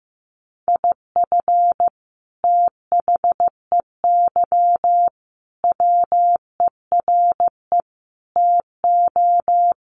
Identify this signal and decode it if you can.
Morse code, heard as a keyed tone, about 15 wpm, 700 Hz. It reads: IF THEY WERE TO
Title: Sentences from Top 1500 Words 15wpm